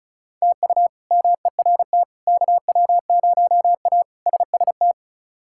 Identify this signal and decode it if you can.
Morse code, heard as a keyed tone, about 36 wpm, 700 Hz. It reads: TU MERT KW0A SST
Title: K1USN SST QSO Practice - 35wpm